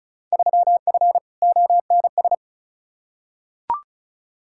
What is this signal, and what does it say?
Morse code, heard as a keyed tone, about 36 wpm, 700 Hz. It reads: 3F ONS
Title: Field Day Exchanges - 35wpm